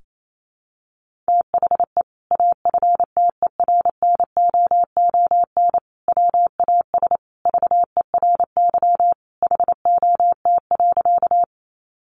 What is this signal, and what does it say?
Morse code, heard as a keyed tone, about 28 wpm, 700 Hz. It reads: THE AFTERNOON WAS VERY HOT.
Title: Sentences from Top 1500 Words 28wpm